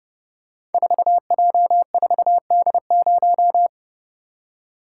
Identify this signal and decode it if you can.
Morse code, heard as a keyed tone, about 30 wpm, 700 Hz. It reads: 4J4D0
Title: Code Groups - 5 Letter:Numbers - 30wpm